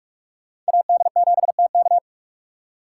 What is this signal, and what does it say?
Morse code, heard as a keyed tone, about 43 wpm, 700 Hz. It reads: AD7TK
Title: USA Call Signs - 2x Character Spacing - 45wpm